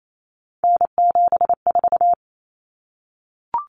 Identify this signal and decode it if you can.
Morse code, heard as a keyed tone, about 28 wpm, 700 Hz. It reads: N74
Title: Common Cut Numbers - 3-Digit 28wpm